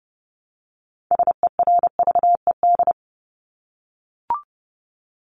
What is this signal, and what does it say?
Morse code, heard as a keyed tone, about 30 wpm, 700 Hz. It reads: SERVED
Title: ED Suffix - Top 100 Words with Base Words 30wpm